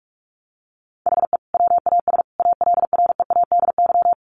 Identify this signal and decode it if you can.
Morse code, heard as a keyed tone, about 44 wpm, 700 Hz. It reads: HE WAS ALREADY
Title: Sentences from Top 1100 Words 45wpm